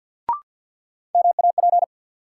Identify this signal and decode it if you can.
Morse code, heard as a keyed tone, about 53 wpm, 700 Hz. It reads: MAP
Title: Sentences from Top 400 Words - New Word Drill - 50wpm